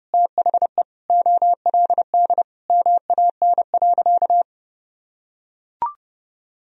Morse code, heard as a keyed tone, about 30 wpm, 700 Hz. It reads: THE OLD MAN.
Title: Sentences from Top 200 Words and Abbreviations 30wpm